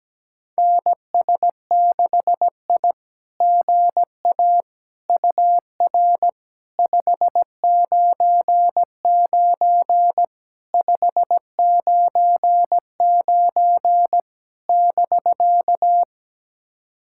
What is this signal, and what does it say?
Morse code, heard as a keyed tone, about 17 wpm, 700 Hz. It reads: NS6I GA UR 599 599 <BK>
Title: SOTA QSO Practice 17wpm